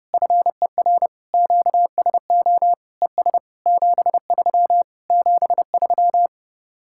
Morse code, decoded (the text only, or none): FER QSO ES 73 73